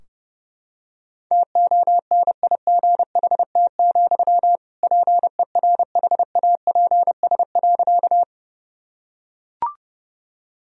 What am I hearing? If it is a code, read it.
Morse code, heard as a keyed tone, about 32 wpm, 700 Hz. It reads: TONIGHT, PERHAPS.
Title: Sentences from Top 1000 Words - Review Part 3 - 30wpm